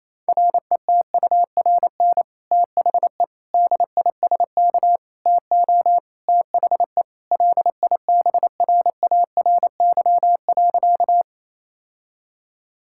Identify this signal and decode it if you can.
Morse code, heard as a keyed tone, about 28 wpm, 700 Hz. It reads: RETURN THE DISK TO THE LIBRARY.
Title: Sentences from Top 1000 Words - Review Part 6 - 28wpm